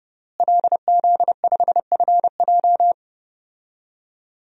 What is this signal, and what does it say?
Morse code, heard as a keyed tone, about 32 wpm, 700 Hz. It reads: LZ5FJ
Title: DXCC-100 Call Signs - 5 Characters - Repeated 2x - 30wpm